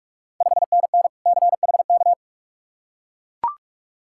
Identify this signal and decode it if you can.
Morse code, heard as a keyed tone, about 44 wpm, 690 Hz. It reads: 5NN CHK
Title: Ontario QSO Party Exchange 45wpm